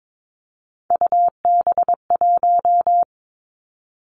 Morse code, heard as a keyed tone, about 22 wpm, 700 Hz. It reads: UB1
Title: Three Challenging Letter-Number Combinations 22wpm